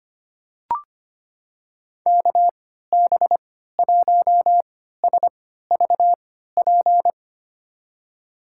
Morse code, heard as a keyed tone, about 25 wpm, spaced as for 14 wpm, 700 Hz. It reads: KB1SVP
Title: USA Call Signs - 6 Characters - 3x Character Spacing - 25wpm